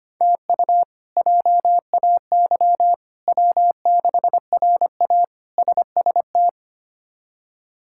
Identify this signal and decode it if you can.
Morse code, heard as a keyed tone, about 25 wpm, 700 Hz. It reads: TU JAY W6RA SST
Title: K1USN SST QSO Practice - 25wpm